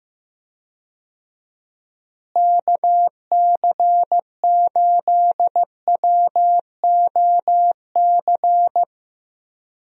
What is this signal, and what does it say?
Morse code, heard as a keyed tone, about 15 wpm, 700 Hz. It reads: KC8WOC
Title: USA Call Signs - 6 Characters - Repeated 2x - 15wpm